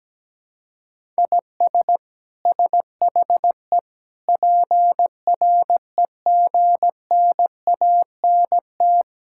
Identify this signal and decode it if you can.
Morse code, heard as a keyed tone, about 17 wpm, 700 Hz. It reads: IS SHE PREGNANT
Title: Sentences from Top 1000 Words - Review Part 6 - 17wpm